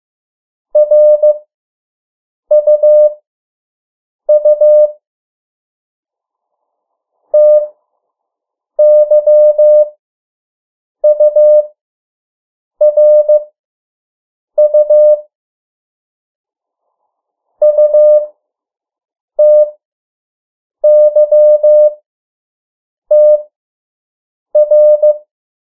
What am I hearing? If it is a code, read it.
Morse code, heard as a keyed tone, about 15 wpm, spaced as for 6 wpm, 600 Hz. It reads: RUU TYURU UTYTR
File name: morse mp3 / 03 urok 2